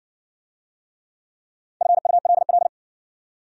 Morse code, heard as a keyed tone, about 52 wpm, 700 Hz. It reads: FULL